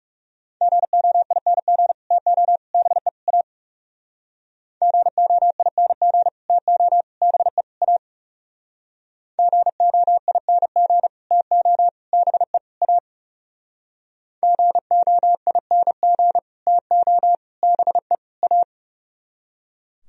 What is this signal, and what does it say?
Morse code, heard as a keyed tone, about 37 wpm, 700 Hz. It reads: GOING TO BE A GOING TO BE A GOING TO BE A GOING TO BE A
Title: Speed-Racing - Sets of 4 Words 30wpm